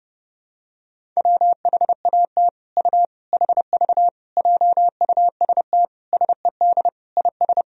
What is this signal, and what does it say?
Morse code, heard as a keyed tone, about 30 wpm, 700 Hz. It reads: WHAT U HV JUST SED IS
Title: Sentences from Top 200 Words and Abbreviations 30wpm